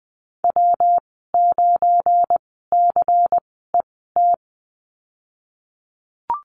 Morse code, heard as a keyed tone, about 20 wpm, spaced as for 15 wpm, 700 Hz. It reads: W9CET
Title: 173 Lesson 28g - Number 9 - Call Signs 20wpm